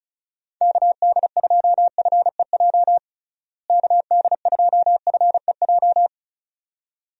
Morse code, heard as a keyed tone, about 36 wpm, 700 Hz. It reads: KD2FEJ KD2FEJ
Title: USA Call Signs - 6 Characters - Repeated 2x - 35wpm